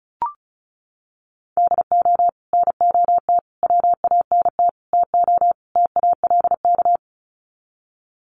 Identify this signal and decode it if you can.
Morse code, heard as a keyed tone, about 34 wpm, 700 Hz. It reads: DO NOT WANT TO TALK
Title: Sets of 5 Words 35wpm